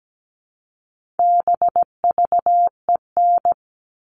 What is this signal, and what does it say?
Morse code, heard as a keyed tone, about 17 wpm, 700 Hz. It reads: BVEN
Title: All Cut Numbers - 4-Digit 17wpm